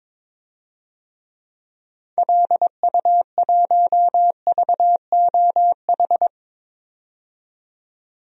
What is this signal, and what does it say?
Morse code, heard as a keyed tone, about 22 wpm, 700 Hz. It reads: LU1VOH
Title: DXCC-100 Call Signs - 6 Characters - 22wpm